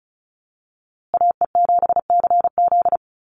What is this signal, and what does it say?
Morse code, heard as a keyed tone, about 34 wpm, 700 Hz. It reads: AE7CZ
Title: USA Call Signs Repeated 3x 35wpm